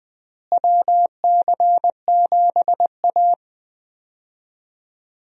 Morse code, heard as a keyed tone, about 21 wpm, 700 Hz. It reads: WC7A
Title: USA Call Signs - 4 Characters - Repeated 2x - 20wpm